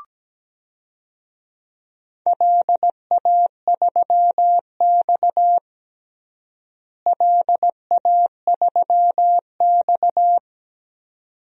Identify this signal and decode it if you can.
Morse code, heard as a keyed tone, about 17 wpm, 700 Hz. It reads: LA3X LA3X
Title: DXCC-100 Call Signs - 4 Characters - Repeated 2x - 17wpm